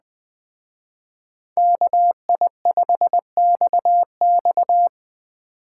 Morse code, heard as a keyed tone, about 20 wpm, 700 Hz. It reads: KI5XX
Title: Speed Racing - USA 2x2 Call Signs 17wpm